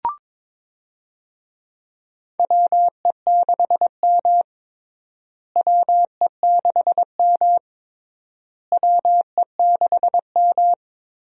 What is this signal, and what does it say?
Morse code, heard as a keyed tone, about 22 wpm, 700 Hz. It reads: WE6M WE6M WE6M
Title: USA Call Signs - 4 Characters - Repeated 3x - 22wpm